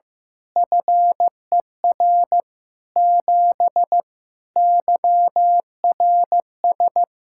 Morse code, heard as a keyed tone, about 15 wpm, 700 Hz. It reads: FER 7 YRS
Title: Practice QSO - Exchange 3 - More Info - 15wpm